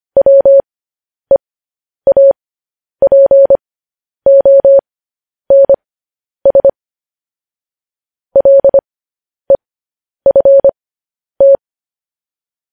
Morse code, heard as a keyed tone, about 25 wpm, spaced as for 10 wpm, 550 Hz. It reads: WEAPONS LEFT
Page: news headlines in morse code at 10 wpm